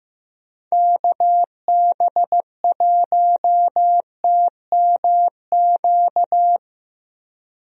Morse code, heard as a keyed tone, about 15 wpm, 700 Hz. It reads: KB1TMQ
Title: USA Call Signs - 6 Characters - Repeated 2x - 15wpm